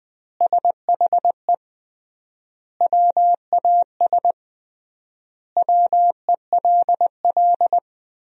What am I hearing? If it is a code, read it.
Morse code, heard as a keyed tone, about 20 wpm, 700 Hz. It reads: SHE WAS WELL
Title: 3x Word-Spacing - Sentences from Top 400 Words - 20wpm